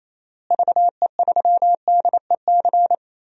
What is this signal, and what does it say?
Morse code, heard as a keyed tone, about 28 wpm, 700 Hz. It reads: VE3DEC
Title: DXCC-100 Call Signs - 6 Characters - Repeated 3x - 28wpm